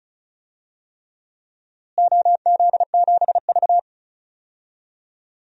Morse code, heard as a keyed tone, about 36 wpm, 700 Hz. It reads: OZ7V
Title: DXCC-100 Call Signs - 4 Characters - Repeated 3x - 35wpm